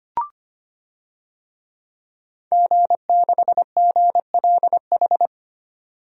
Morse code, heard as a keyed tone, about 25 wpm, 700 Hz. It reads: G6GLH
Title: DXCC-100 Call Signs - 5 Characters - 25wpm